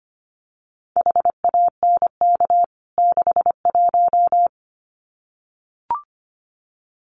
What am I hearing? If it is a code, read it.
Morse code, heard as a keyed tone, about 25 wpm, 700 Hz. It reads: HANK 61
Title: CWT Exchange 25wpm